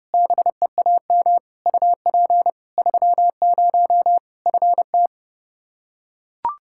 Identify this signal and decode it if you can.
Morse code, heard as a keyed tone, about 32 wpm, 700 Hz. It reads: BEAM UP 30 FT
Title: Practice QSO - Exchange 2 - Info - 30wpm